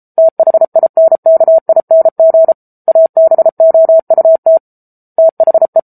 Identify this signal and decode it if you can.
Morse code, heard as a keyed tone, about 34 wpm, 650 Hz. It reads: THINKING ABOUT THE